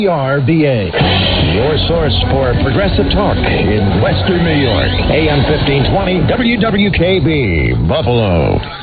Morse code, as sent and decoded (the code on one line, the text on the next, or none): none